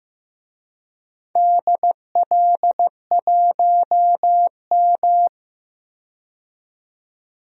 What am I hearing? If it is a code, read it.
Morse code, heard as a keyed tone, about 15 wpm, 700 Hz. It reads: DL1M
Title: DXCC-100 Call Signs - 4 Characters - 15wpm